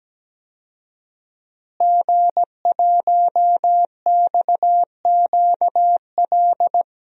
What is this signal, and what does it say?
Morse code, heard as a keyed tone, about 17 wpm, 700 Hz. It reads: G1XQL
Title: DXCC-100 Call Signs - 5 Characters - Repeated 3x - 17wpm